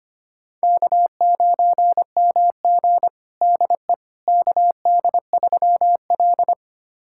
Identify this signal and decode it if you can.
Morse code, heard as a keyed tone, about 25 wpm, 700 Hz. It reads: K9MG DE KD3L
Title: USA Callsign DE Callsign - 4 Characters - 25wpm